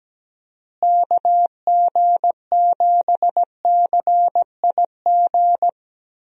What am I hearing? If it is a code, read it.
Morse code, heard as a keyed tone, about 17 wpm, 700 Hz. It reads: KG7CIG
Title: USA Call Signs - 6 Characters - Repeated 2x - 17wpm